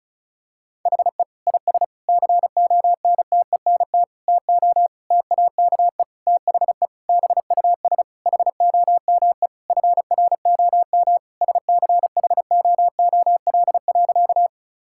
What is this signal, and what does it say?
Morse code, heard as a keyed tone, about 34 wpm, 700 Hz. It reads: HE IS CONTENT TO TAKE THE BUS HOME FROM SCHOOL.
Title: Sentences from Top 1000 Words - Review Part 6 - 35wpm